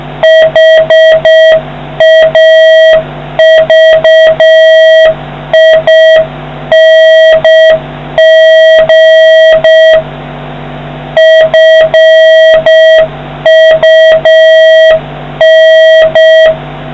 Morse code, heard as a keyed tone, about 6 wpm, 1900 Hz, with noise. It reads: HAVING FUN